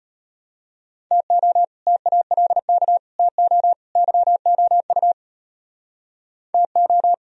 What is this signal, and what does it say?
Morse code, heard as a keyed tone, about 38 wpm, 700 Hz. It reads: TO TALK TO YOU TO